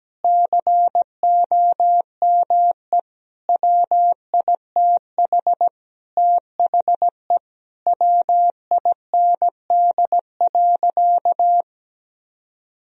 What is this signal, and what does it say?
Morse code, heard as a keyed tone, about 17 wpm, 700 Hz. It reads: COME WITH THE WIND.
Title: Sentences from Top 1000 Words - Review Part 2 - 17wpm